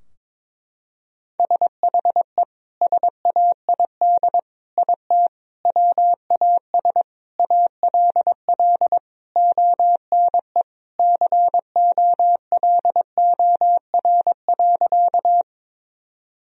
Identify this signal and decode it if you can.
Morse code, heard as a keyed tone, about 22 wpm, 700 Hz. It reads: SHE SAID IT WAS ALL ONE COLOR.
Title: Sentences from Top 1000 Words - Review Part 2 - 22wpm